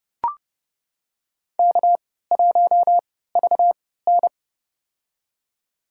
Morse code, heard as a keyed tone, about 30 wpm, spaced as for 17 wpm, 700 Hz. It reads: K1VN
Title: USA Call Signs - 4 Characters - 3x Character Spacing - 30wpm